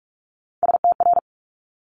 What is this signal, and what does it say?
Morse code, heard as a keyed tone, about 46 wpm, 690 Hz. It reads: STR
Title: Top Three Letter Combinations in Words 45wpm